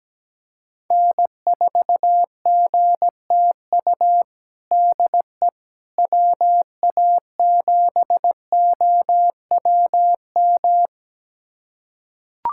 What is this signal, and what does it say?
Morse code, heard as a keyed tone, about 17 wpm, 700 Hz. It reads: N4GTU DE WA7OWM
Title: USA Callsign DE Callsign 17wpm